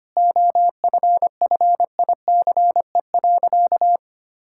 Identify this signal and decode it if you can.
Morse code, heard as a keyed tone, about 25 wpm, 700 Hz. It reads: OFFICE.
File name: practice / 25wpm / Sentences with 3-Word Phrases - 25wpm